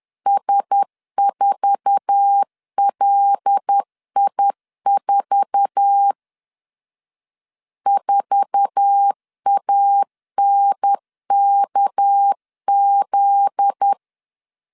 Morse code, heard as a keyed tone, about 10 wpm, 800 Hz. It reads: S4LI4 4ANKZ
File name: LEZIONI / 8 YJ45 / 8 YJ4 10wpm 2633